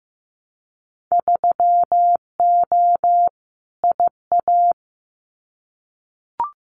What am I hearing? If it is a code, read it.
Morse code, heard as a keyed tone, about 15 wpm, 700 Hz. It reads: 3O IA
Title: Winter Field Day Exchanges 15wpm